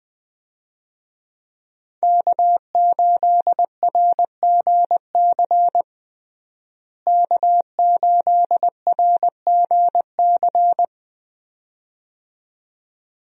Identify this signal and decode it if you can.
Morse code, heard as a keyed tone, about 21 wpm, 700 Hz. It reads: K8RGC K8RGC
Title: USA Call Signs - 5 Characters - Repeated 2x - 20wpm